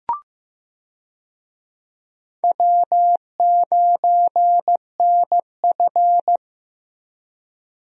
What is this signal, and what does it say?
Morse code, heard as a keyed tone, about 15 wpm, 700 Hz. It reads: W9NF
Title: USA Call Signs - 4 Characters - Repeated 2x - 15wpm